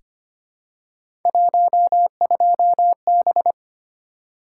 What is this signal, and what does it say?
Morse code, heard as a keyed tone, about 25 wpm, 700 Hz. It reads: 12B
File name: Three Letter-Number Combinations - 25wpm